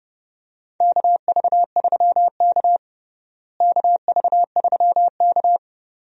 Morse code, heard as a keyed tone, about 30 wpm, 700 Hz. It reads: KV3K KV3K
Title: USA Call Signs - 4 Characters - Repeated 2x - 30wpm